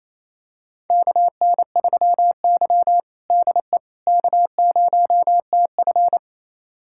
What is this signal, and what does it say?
Morse code, heard as a keyed tone, about 28 wpm, 700 Hz. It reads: KN3Y DE K0TF
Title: USA Callsign DE Callsign - 4 Characters - 28wpm